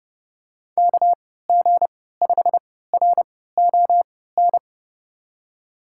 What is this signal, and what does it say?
Morse code, heard as a keyed tone, about 31 wpm, spaced as for 17 wpm, 700 Hz. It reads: KG5RON
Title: USA Call Signs - 3x Character Spacing - 30wpm